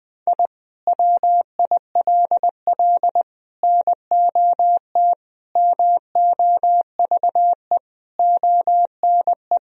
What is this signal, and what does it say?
Morse code, heard as a keyed tone, about 20 wpm, 700 Hz. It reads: I WILL NOT MOVE ONE
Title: Sentences from Top 1000 Words - Review Part 3 - 20wpm